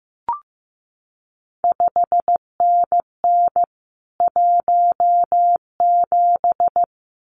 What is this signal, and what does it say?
Morse code, heard as a keyed tone, about 15 wpm, 700 Hz. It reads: 5NN 17
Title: CQ WW DX Contest Exchange Encore - 15wpm